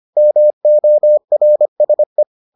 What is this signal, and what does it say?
Morse code, heard as a keyed tone, about 25 wpm, 600 Hz. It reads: MORSE